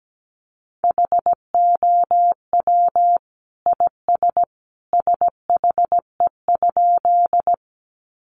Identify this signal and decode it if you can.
Morse code, heard as a keyed tone, about 17 wpm, 700 Hz. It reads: HOW IS SHE?
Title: Sentences from Top 100 Words 17wpm